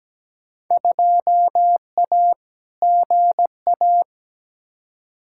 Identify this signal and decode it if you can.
Morse code, heard as a keyed tone, about 17 wpm, 700 Hz. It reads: 2A GA
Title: Field Day Exchanges Encore - 17wpm